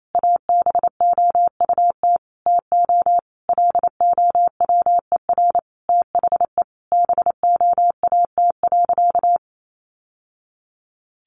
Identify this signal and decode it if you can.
Morse code, heard as a keyed tone, about 28 wpm, 700 Hz. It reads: ABOUT TO LOWER THE BOAT.
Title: Sentences from Top 700 Words 28wpm